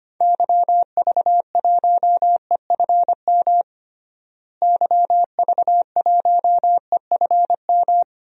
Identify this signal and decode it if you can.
Morse code, heard as a keyed tone, about 25 wpm, 700 Hz. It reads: YV1EFM YV1EFM
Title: DXCC-100 Call Signs - 6 Characters - Repeated 3x - 25wpm